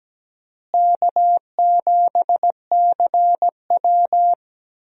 Morse code, heard as a keyed tone, about 17 wpm, 700 Hz. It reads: K7CW
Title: USA Call Signs - 4 Characters - Repeated 2x - 17wpm